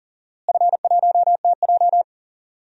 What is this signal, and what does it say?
Morse code, heard as a keyed tone, about 40 wpm, 700 Hz. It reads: F1TJ
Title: DXCC-100 Call Signs - 4 Characters - Repeated 2x - 40wpm